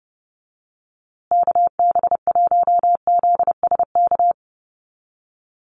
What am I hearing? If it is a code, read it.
Morse code, heard as a keyed tone, about 30 wpm, 700 Hz. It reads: KB1ZSK
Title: USA Call Signs Repeated 2x 30wpm